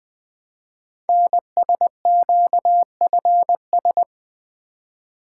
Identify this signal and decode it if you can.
Morse code, heard as a keyed tone, about 20 wpm, 700 Hz. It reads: NSQFS
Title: Code Groups - 5 Letters - 20wpm